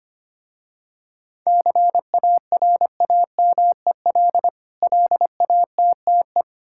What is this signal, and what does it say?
Morse code, heard as a keyed tone, about 25 wpm, 700 Hz. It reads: CARAMEL LATTE
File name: practice / 25wpm / Difficult Words with Hints - 25wpm